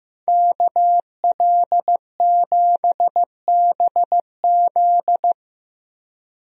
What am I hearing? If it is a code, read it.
Morse code, heard as a keyed tone, about 15 wpm, 700 Hz. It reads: KL7BZ
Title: DXCC-100 Call Signs - 5 Characters - Repeated 2x - 15wpm